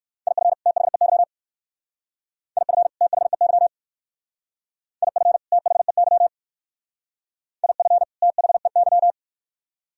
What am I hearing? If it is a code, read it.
Morse code, heard as a keyed tone, about 55 wpm, 700 Hz. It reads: IF THEY IF THEY IF THEY IF THEY
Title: Speed-Racing - Sets of 2 Words 45wpm